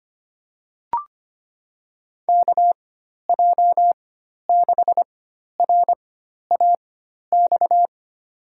Morse code, heard as a keyed tone, about 25 wpm, spaced as for 12 wpm, 700 Hz. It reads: KJ6RAX
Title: USA Call Signs - 6 Characters - 4x Character Spacing - 25wpm